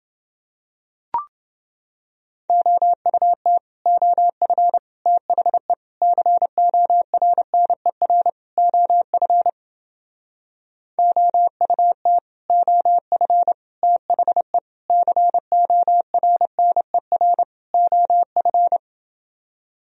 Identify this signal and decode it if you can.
Morse code, heard as a keyed tone, about 30 wpm, 700 Hz. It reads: OUT OF THE CORNER OF OUT OF THE CORNER OF